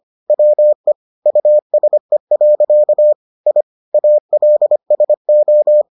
Morse code, heard as a keyed tone, about 25 wpm, 600 Hz. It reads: WE USE. I ALSO